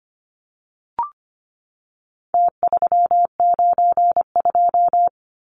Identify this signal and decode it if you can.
Morse code, heard as a keyed tone, about 25 wpm, 700 Hz. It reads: T392
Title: Numbers - Contest Style - 4-Digit 25wpm